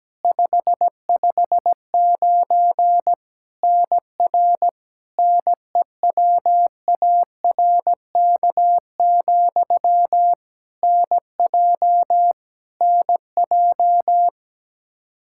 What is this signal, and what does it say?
Morse code, heard as a keyed tone, about 17 wpm, 700 Hz. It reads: 559 NR NEWARK, NJ NJ
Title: K1USN SST QSO Practice 17wpm